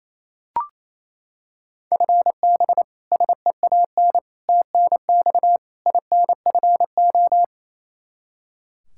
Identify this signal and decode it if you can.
Morse code, heard as a keyed tone, about 27 wpm, 700 Hz. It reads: FB SEAN TNX INFO
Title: Practice QSO - Exchange 3 - More Info - 28wpm